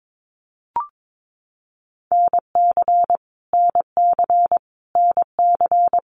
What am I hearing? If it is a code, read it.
Morse code, heard as a keyed tone, about 22 wpm, 700 Hz. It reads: NC NC NC
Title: US States Repeated 3x 22wpm